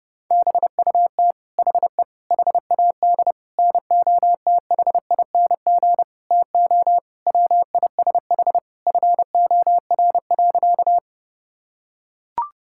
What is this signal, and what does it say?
Morse code, heard as a keyed tone, about 30 wpm, 700 Hz. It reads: BUT HE HAD NOTHING TO WISH FOR.
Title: Sentences from Top 1600 Words 30wpm